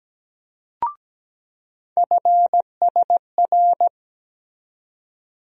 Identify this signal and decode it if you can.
Morse code, heard as a keyed tone, about 17 wpm, 700 Hz. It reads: FSR
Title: USA Call Sign Postfixes - 3 Letters 17wpm